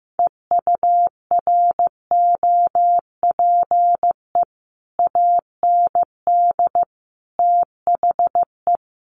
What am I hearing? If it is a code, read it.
Morse code, heard as a keyed tone, about 15 wpm, 700 Hz. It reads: EUROPE AND THE
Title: Sets of 5 Words Encore2 15wpm